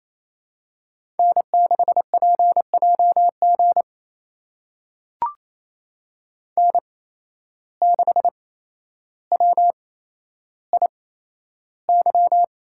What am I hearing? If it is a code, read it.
Morse code, heard as a keyed tone, about 28 wpm, spaced as for 12 wpm, 700 Hz. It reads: N6PJG N6WIY
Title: USA Call Signs - 5 Characters - 8x Character Spacing - 28wpm